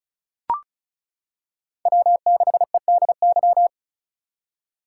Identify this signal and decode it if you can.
Morse code, heard as a keyed tone, about 36 wpm, 700 Hz. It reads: W6EDY
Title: K1USN SST QSO Practice - 35wpm